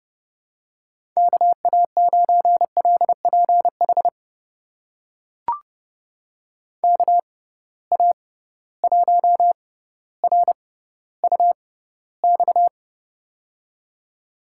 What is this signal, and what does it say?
Morse code, heard as a keyed tone, about 30 wpm, spaced as for 15 wpm, 700 Hz. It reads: KA9LPH KA1RUX
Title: USA Call Signs - 6 Characters - 6x Character Spacing - 30wpm